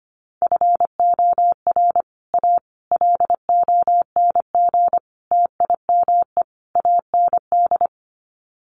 Morse code, heard as a keyed tone, about 25 wpm, 700 Hz. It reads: FOR A LONG TIME AND
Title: Sets of 5 Words Encore2 25wpm